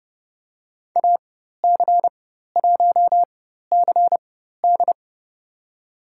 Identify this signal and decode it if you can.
Morse code, heard as a keyed tone, about 30 wpm, spaced as for 14 wpm, 700 Hz. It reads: AC1CD
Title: USA Call Signs - 5 Characters - 4x Character Spacing - 30wpm